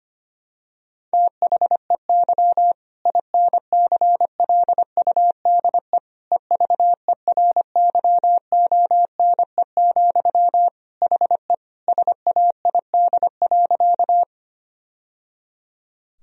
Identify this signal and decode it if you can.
Morse code, heard as a keyed tone, about 25 wpm, 700 Hz. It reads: THEY INCLUDE EVERYONE, HE SAID.
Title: Sentences from Top 1000 Words - Review Part 3 - 25wpm